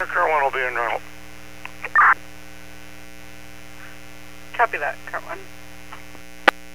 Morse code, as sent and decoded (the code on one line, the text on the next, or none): none